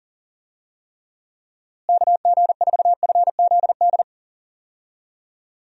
Morse code, heard as a keyed tone, about 40 wpm, 700 Hz. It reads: KG4FZD
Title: USA Call Signs - 6 Characters - Repeated 3x - 40wpm